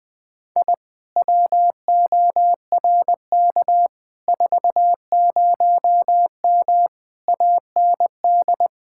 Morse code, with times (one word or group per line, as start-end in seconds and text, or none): 0.6-0.7 I
1.2-3.9 WORK
4.3-6.9 40M
7.3-8.7 AND